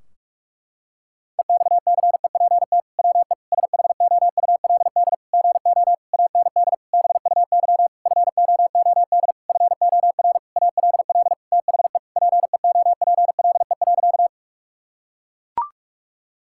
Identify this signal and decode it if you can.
Morse code, heard as a keyed tone, about 41 wpm, 700 Hz. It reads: EXCEPT WE SHOULD GO AND BUY FOOD FOR ALL THE PEOPLE.
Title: Sentences from Top 1600 Words 45wpm